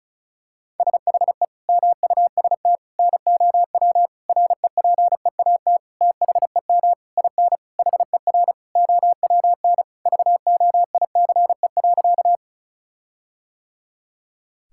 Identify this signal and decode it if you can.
Morse code, heard as a keyed tone, about 36 wpm, 700 Hz. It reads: SHE MUST NOW REPEAT THEM IN HER OWN VOICE.
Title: Sentences from Top 1600 Words 35wpm